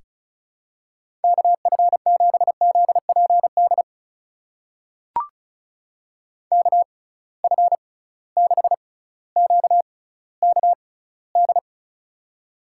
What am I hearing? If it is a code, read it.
Morse code, heard as a keyed tone, about 36 wpm, spaced as for 18 wpm, 700 Hz. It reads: KF7ZPD KF6QKD
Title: USA Call Signs - 6 Characters - 6x Character Spacing - 35wpm